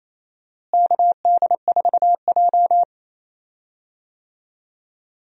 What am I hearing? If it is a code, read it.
Morse code, heard as a keyed tone, about 28 wpm, 700 Hz. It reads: KD4J
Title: USA Call Signs - 4 Characters - Repeated 2x - 28wpm